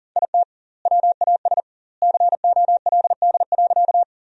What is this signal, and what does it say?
Morse code, heard as a keyed tone, about 40 wpm, 700 Hz. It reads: IT WAS COLD.
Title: 2x Word-Spacing - Sentences from Top 400 Words - 40wpm